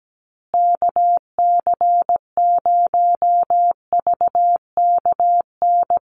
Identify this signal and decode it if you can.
Morse code, heard as a keyed tone, about 17 wpm, 700 Hz. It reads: KC0VKN
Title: Call Signs Repeated 3x - 17wpm